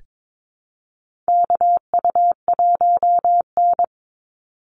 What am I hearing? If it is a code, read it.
Morse code, heard as a keyed tone, about 22 wpm, 700 Hz. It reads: KU1N
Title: Call Signs Repeated 2x - 22wpm